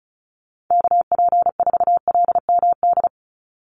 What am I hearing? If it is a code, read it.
Morse code, heard as a keyed tone, about 36 wpm, 700 Hz. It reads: KP4LMD
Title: USA Call Signs 35wpm